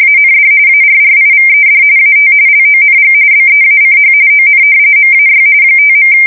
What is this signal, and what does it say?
RTTY, about 45.45 baud. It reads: AND ARRIVED AT LONG BEACH ON MAY 5.
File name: fsk170 / battleshipnj